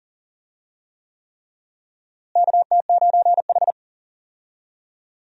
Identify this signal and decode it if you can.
Morse code, heard as a keyed tone, about 40 wpm, 700 Hz. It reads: KT9H
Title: USA Call Signs - 4 Characters - Repeated 3x - 40wpm